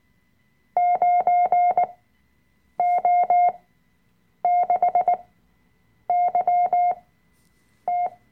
Morse code, heard as a keyed tone, about 19 wpm, spaced as for 7 wpm, 720 Hz. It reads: 9O6YT
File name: Workshop Dateien / CW-Online-Kurs / Lektion 005